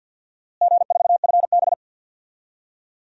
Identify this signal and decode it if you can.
Morse code, heard as a keyed tone, about 52 wpm, 700 Hz. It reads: G4FB
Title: DXCC-100 Call Signs - 4 Characters - Repeated 2x - 50wpm